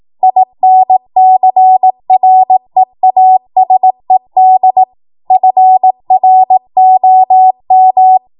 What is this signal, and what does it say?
Morse code, heard as a keyed tone, about 18 wpm, 750 Hz. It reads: INCREASED FROM
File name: Morse / 18WPM